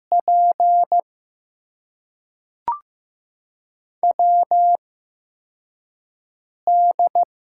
Morse code, heard as a keyed tone, about 15 wpm, spaced as for 4 wpm, 700 Hz. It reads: P WD
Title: USA Call Signs - 8x Character Spacing - 15wpm